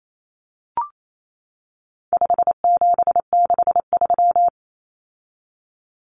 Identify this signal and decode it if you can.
Morse code, heard as a keyed tone, about 28 wpm, 700 Hz. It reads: 5763
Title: Numbers - 4-Digit 28wpm